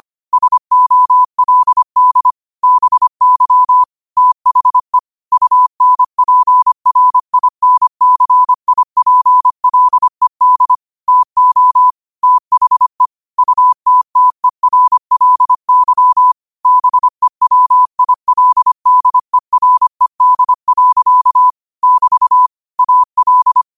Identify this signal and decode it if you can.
Morse code, heard as a keyed tone, about 25 wpm, 1000 Hz. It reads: SOLD BY THE UNPRINCIPLED TO THE UTTERLY BEWILDERED. <BT> AL
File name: qotdfiles / 25wpm